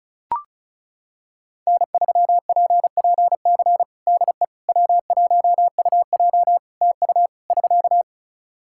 Morse code, heard as a keyed tone, about 35 wpm, 700 Hz. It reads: N3PPC DE W1UJ TU <SK>
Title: Practice QSO - Exchange 4 - Ending - 35wpm